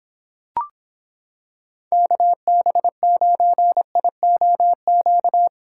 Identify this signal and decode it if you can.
Morse code, heard as a keyed tone, about 26 wpm, 700 Hz. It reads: KB9IOQ